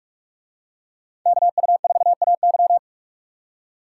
Morse code, heard as a keyed tone, about 44 wpm, 700 Hz. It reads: KU4AY